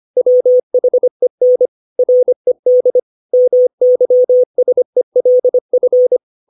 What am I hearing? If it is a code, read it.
Morse code, heard as a keyed tone, about 25 wpm, 500 Hz. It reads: WHEN RED MYSELF
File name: 632 words / 632 Words 25WPM